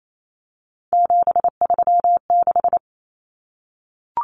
Morse code, heard as a keyed tone, about 28 wpm, 700 Hz. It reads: 736
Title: Common Cut Numbers - 3-Digit 28wpm